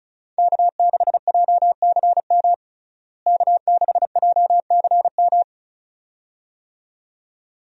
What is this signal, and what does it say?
Morse code, heard as a keyed tone, about 36 wpm, 700 Hz. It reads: K6JCM K6JCM
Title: USA Call Signs - 5 Characters - Repeated 2x - 35wpm